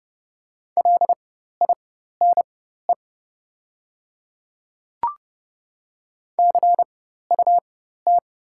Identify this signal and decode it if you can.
Morse code, heard as a keyed tone, about 31 wpm, spaced as for 14 wpm, 700 Hz. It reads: LINE CUT
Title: 4x Character Spacing - Top 300 Words - 30wpm